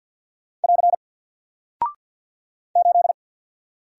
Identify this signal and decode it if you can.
Morse code, heard as a keyed tone, about 49 wpm, 700 Hz. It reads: <AR> 8
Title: Single Character (Letter:Number:Punctuation:Prosign) 50wpm